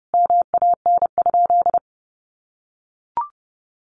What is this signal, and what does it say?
Morse code, heard as a keyed tone, about 30 wpm, 700 Hz. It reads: MAN?
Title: Sentences from Top 1500 Words 30wpm